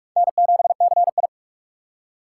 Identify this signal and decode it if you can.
Morse code, heard as a keyed tone, about 39 wpm, 700 Hz. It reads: N7CI